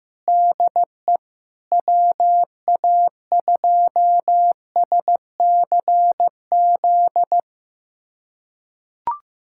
Morse code, heard as a keyed tone, about 15 wpm, 700 Hz. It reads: DE WA2SCZ
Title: USA Callsign DE Callsign - 6 Characters - 15wpm